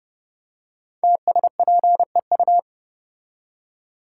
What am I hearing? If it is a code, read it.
Morse code, heard as a keyed tone, about 30 wpm, 700 Hz. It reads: TSPEU
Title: Code Groups - 5 Letter:Numbers - 30wpm